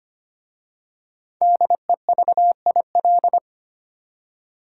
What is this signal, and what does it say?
Morse code, heard as a keyed tone, about 25 wpm, 700 Hz. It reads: DEVIL
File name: 5 Letter Words - First 3 Letters Given - 25wpm